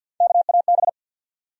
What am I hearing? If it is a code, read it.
Morse code, heard as a keyed tone, about 47 wpm, 700 Hz. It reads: KAB
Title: USA Call Sign Postfixes - 3 Letters 50wpm